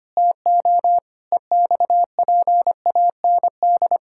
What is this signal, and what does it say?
Morse code, heard as a keyed tone, about 25 wpm, 700 Hz. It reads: TO EXPAND
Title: Pangram Challenge v2 - 25wpm